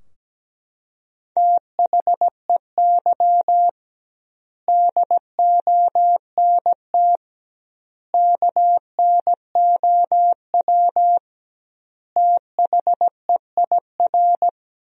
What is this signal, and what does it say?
Morse code, heard as a keyed tone, about 17 wpm, 700 Hz. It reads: THEY DONT KNOW THEIR
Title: 2x Word-Spacing - Sentences from Top 300 Words - 17wpm